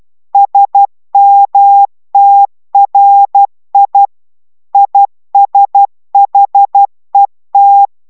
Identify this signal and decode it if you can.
Morse code, heard as a keyed tone, about 12 wpm, 800 Hz. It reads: SMTRI ISHET